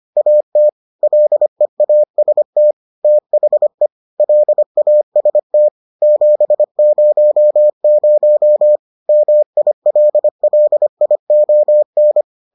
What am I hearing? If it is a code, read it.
Morse code, heard as a keyed tone, about 25 wpm, 600 Hz. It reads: AT LEAST THE LAST 700 MILLION